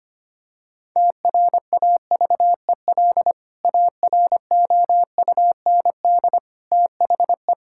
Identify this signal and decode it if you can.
Morse code, heard as a keyed tone, about 25 wpm, 700 Hz. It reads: TRAVEL AROUND THE
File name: practice / 25wpm / Sentences with 3-Word Phrases - 25wpm